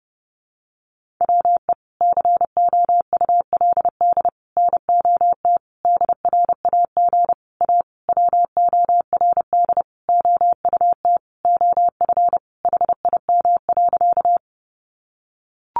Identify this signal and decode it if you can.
Morse code, heard as a keyed tone, about 30 wpm, 700 Hz. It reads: WE COULD NOT DRAG A WORD OUT OF HIM.
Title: Sentences from Top 1500 Words 30wpm